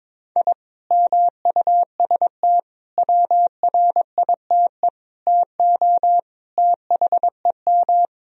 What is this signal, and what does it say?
Morse code, heard as a keyed tone, about 22 wpm, 700 Hz. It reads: I MUST WRITE TO THEM
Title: Sentences from Top 1000 Words - Review Part 1 - 22wpm